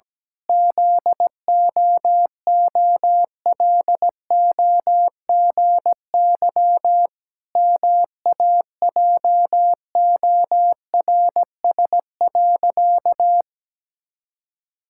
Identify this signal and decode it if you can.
Morse code, heard as a keyed tone, about 17 wpm, 700 Hz. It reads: ZOOLOGY MAJORS.
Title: Pangram Challenge v2 - 17wpm